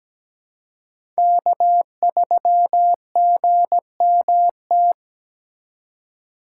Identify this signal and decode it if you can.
Morse code, heard as a keyed tone, about 17 wpm, 700 Hz. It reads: K3GMT
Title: K1USN SST QSO Practice - 17wpm